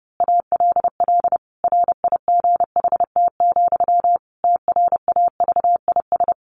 Morse code, heard as a keyed tone, about 30 wpm, 700 Hz. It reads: ALL RIGHT, TRAVIS
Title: A Sound Of Thunder 30wpm